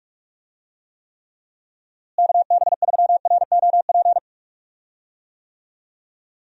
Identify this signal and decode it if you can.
Morse code, heard as a keyed tone, about 48 wpm, 700 Hz. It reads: KB3ROP
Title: USA Call Signs - 6 Characters - Repeated 3x - 45wpm